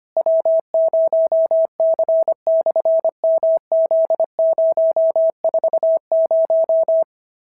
Morse code, heard as a keyed tone, about 25 wpm, 650 Hz. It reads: W0C/MZ040
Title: SOTA QSO Practice 25wpm